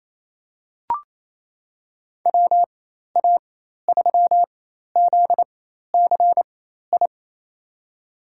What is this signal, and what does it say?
Morse code, heard as a keyed tone, about 28 wpm, spaced as for 13 wpm, 700 Hz. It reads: WA3ZCI
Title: USA Call Signs - 4x Character Spacing - 28wpm